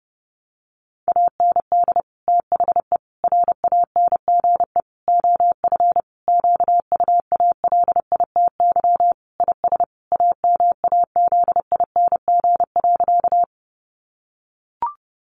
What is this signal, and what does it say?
Morse code, heard as a keyed tone, about 28 wpm, 700 Hz. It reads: AND THE RANGE OF QUALITY IS AMAZING.
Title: Sentences from Top 1500 Words 30wpm